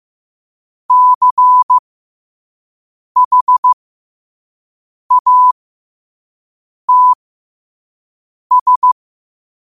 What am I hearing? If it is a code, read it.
Morse code, heard as a keyed tone, about 15 wpm, spaced as for 5 wpm, 1000 Hz. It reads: CHATS
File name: qotdfiles / 5wpm